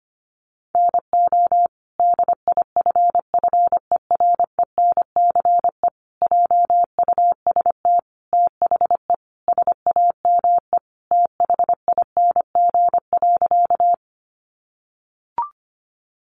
Morse code, heard as a keyed tone, about 25 wpm, 700 Hz. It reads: NO DIFFERENCE JUST THE SAME THING.
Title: Sentences from Top 800 Words 25wpm